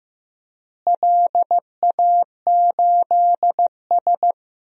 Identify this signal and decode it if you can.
Morse code, heard as a keyed tone, about 15 wpm, 700 Hz. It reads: LA8S
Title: DXCC-100 Call Signs - 4 Characters - Repeated 2x - 15wpm